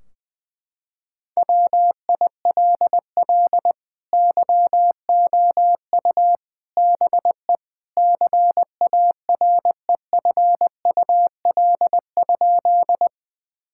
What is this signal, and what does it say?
Morse code, heard as a keyed tone, about 20 wpm, 700 Hz. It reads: WILL YOU BE CAREFUL?
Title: Sentences from Top 1600 Words 20wpm